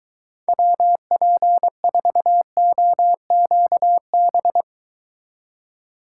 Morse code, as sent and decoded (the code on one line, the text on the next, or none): .-- .--. ....- --- --.- -...
WP4OQB